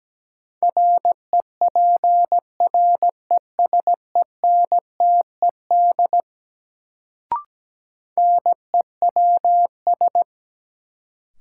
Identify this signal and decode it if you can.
Morse code, heard as a keyed tone, about 17 wpm, 700 Hz. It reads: REPRESENTED NEWS
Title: Sets of 5 Words - New Word Drill - 17wpm